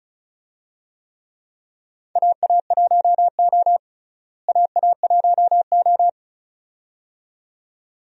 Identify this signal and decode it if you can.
Morse code, heard as a keyed tone, about 36 wpm, 700 Hz. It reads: AA1O AA1O
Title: USA Call Signs - 4 Characters - Repeated 2x - 35wpm